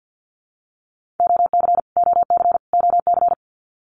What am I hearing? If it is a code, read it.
Morse code, heard as a keyed tone, about 54 wpm, 700 Hz. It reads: OC OC OC
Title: Abbreviations Repeated 3x - 50wpm